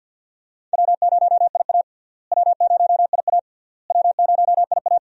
Morse code, heard as a keyed tone, about 51 wpm, 700 Hz. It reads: W0IA W0IA W0IA
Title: USA Call Signs - 4 Characters - Repeated 3x - 50wpm